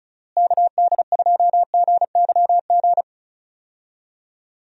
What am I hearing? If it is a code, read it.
Morse code, heard as a keyed tone, about 34 wpm, 700 Hz. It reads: KD2GYG
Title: USA Call Signs - 6 Characters - Repeated 2x - 35wpm